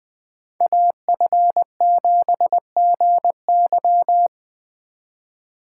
Morse code, heard as a keyed tone, about 20 wpm, 700 Hz. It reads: AF7GY
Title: Speed Racing - USA 2x2 Call Signs 17wpm